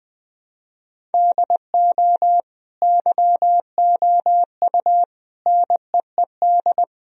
Morse code, heard as a keyed tone, about 21 wpm, 700 Hz. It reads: DO YOU NEED
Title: Sentences from Top 1000 Words - Review Part 1 - 20wpm